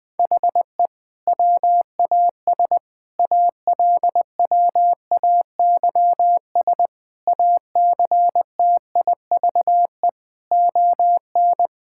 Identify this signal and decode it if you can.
Morse code, heard as a keyed tone, about 21 wpm, 700 Hz. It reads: HE WAS ALWAYS ACTIVE ON
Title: Sentences from Top 1000 Words - Review Part 6 - 20wpm